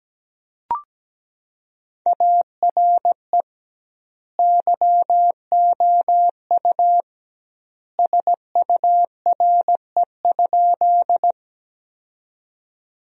Morse code, heard as a keyed tone, about 17 wpm, 700 Hz. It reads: ARE YOU SURE?
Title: 2x Word-Spacing - Sentences from Top 300 Words - 17wpm